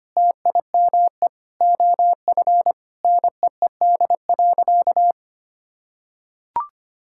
Morse code, heard as a keyed tone, about 25 wpm, 700 Hz. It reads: TIME OF NEED.
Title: Sentences from Top 1000 Words - Review Part 5 - 25wpm